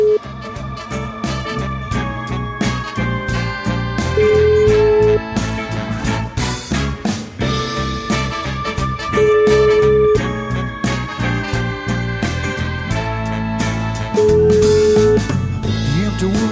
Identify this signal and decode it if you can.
Ringback tone (Europe, 425 Hz), with noise.